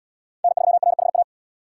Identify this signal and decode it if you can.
Morse code, heard as a keyed tone, about 68 wpm, 700 Hz. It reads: N3RHA